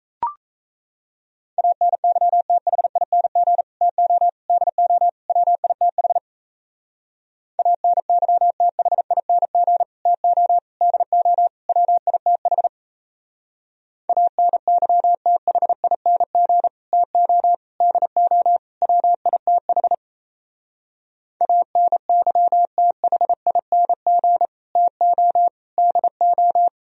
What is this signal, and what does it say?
Morse code, heard as a keyed tone, about 36 wpm, 700 Hz. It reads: ANYTHING TO DO WITH ANYTHING TO DO WITH ANYTHING TO DO WITH ANYTHING TO DO